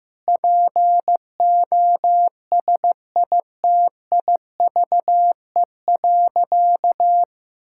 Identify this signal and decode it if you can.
Morse code, heard as a keyed tone, about 15 wpm, 700 Hz. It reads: POSITIVE.
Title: Sentences from Top 1000 Words - Review Part 5 - 15wpm